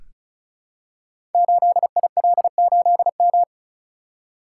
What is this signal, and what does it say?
Morse code, heard as a keyed tone, about 36 wpm, 690 Hz. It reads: 8IL8M
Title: Code Groups - 5 Letter:Numbers - 35wpm